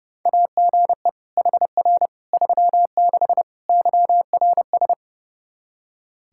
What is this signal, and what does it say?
Morse code, heard as a keyed tone, about 32 wpm, 700 Hz. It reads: AGE HR 36 YRS
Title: Practice QSO - Exchange 3 - More Info - 30wpm